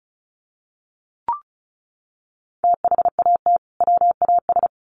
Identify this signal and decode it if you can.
Morse code, heard as a keyed tone, about 34 wpm, 700 Hz. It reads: THAT WAS
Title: Sets of 2 Words 35wpm